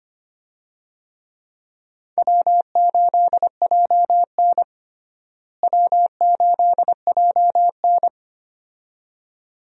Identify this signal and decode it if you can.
Morse code, heard as a keyed tone, about 25 wpm, 700 Hz. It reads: W8JN W8JN
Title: USA Call Signs - 4 Characters - Repeated 2x - 25wpm